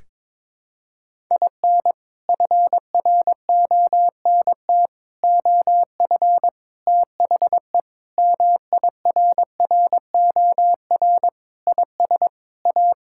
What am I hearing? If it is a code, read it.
Morse code, heard as a keyed tone, about 22 wpm, 700 Hz. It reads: IN FRONT OF THE MIRROR IS A
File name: practice / 22wpm / Sentences with 4-Word Phrases - 22wpm